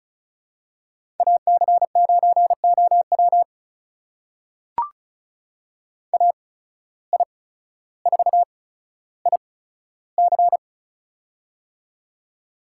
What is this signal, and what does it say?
Morse code, heard as a keyed tone, about 34 wpm, spaced as for 15 wpm, 700 Hz. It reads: AC9OW AI4IC
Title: USA Call Signs - 5 Characters - 8x Character Spacing - 35wpm